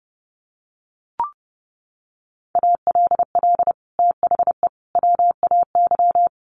Morse code, heard as a keyed tone, about 30 wpm, 700 Hz. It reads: ALL THE WAY
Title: Sets of 3 Words 30wpm